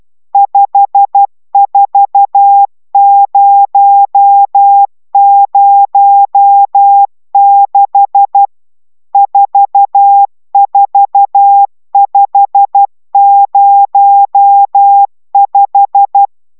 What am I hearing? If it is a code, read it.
Morse code, heard as a keyed tone, about 12 wpm, 800 Hz. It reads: 54006 44505
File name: taviro tanulas / cw 69 20 10